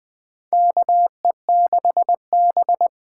Morse code, heard as a keyed tone, about 21 wpm, 700 Hz. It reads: KE6B
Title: USA Call Signs - 6 Characters - 8x Character Spacing - 20wpm